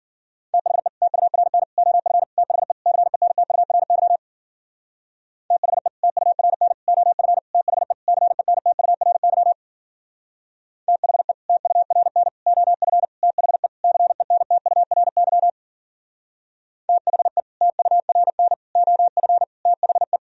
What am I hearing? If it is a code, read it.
Morse code, heard as a keyed tone, about 49 wpm, 700 Hz. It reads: THE TURN OF THE CENTURY THE TURN OF THE CENTURY THE TURN OF THE CENTURY THE TURN OF THE
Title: Speed-Racing - Sets of 5 Words 40wpm